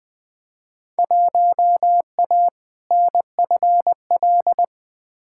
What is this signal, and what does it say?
Morse code, heard as a keyed tone, about 20 wpm, 700 Hz. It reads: 1A NFL
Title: Field Day Exchanges Encore - 20wpm